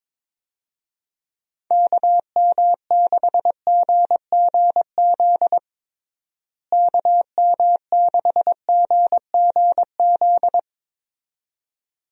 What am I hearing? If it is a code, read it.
Morse code, heard as a keyed tone, about 22 wpm, 700 Hz. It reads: KM6GGZ KM6GGZ
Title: USA Call Signs - 6 Characters - Repeated 2x - 22wpm